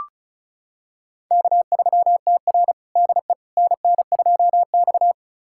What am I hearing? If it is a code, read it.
Morse code, heard as a keyed tone, about 36 wpm, 700 Hz. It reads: K3TR DE NN2X
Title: USA Callsign DE Callsign - 4 Characters - 35wpm